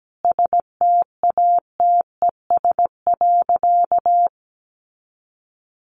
Morse code, heard as a keyed tone, about 17 wpm, 700 Hz. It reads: STATES.
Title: A Sound Of Thunder 17wpm